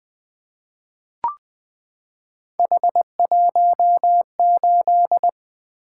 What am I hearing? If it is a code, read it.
Morse code, heard as a keyed tone, about 20 wpm, 700 Hz. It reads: H18
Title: Three Letter-Number Combinations - 20wpm